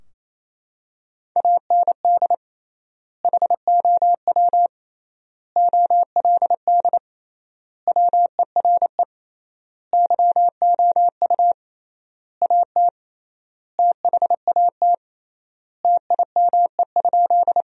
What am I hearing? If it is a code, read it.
Morse code, heard as a keyed tone, about 28 wpm, 700 Hz. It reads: AND HOW OLD WERE YOU AT THAT TIME?
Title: 3x Word-Spacing - Sentences from Top 200 Words - 28wpm